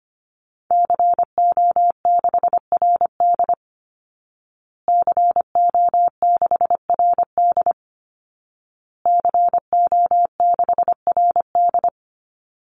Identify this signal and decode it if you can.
Morse code, heard as a keyed tone, about 25 wpm, 700 Hz. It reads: CO6RD CO6RD CO6RD
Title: Call Signs Repeated 2x - 25wpm